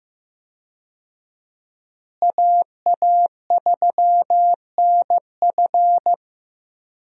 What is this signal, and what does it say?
Morse code, heard as a keyed tone, about 15 wpm, 700 Hz. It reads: AA3NF
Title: USA Call Signs - 5 Characters - Repeated 3x - 15wpm